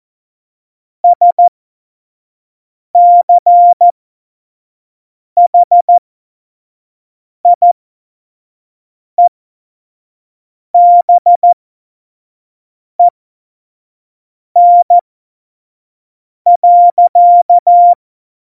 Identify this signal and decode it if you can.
Morse code, heard as a keyed tone, about 14 wpm, spaced as for 5 wpm, 700 Hz. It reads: SCHIEBEN.
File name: Morsekurs / Witze / Witz 26. Februar 2019, 25 F70